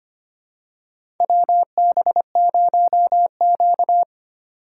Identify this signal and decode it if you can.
Morse code, heard as a keyed tone, about 25 wpm, 700 Hz. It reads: WB0Q
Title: USA Call Signs - 4 Characters - 6x Character Spacing - 25wpm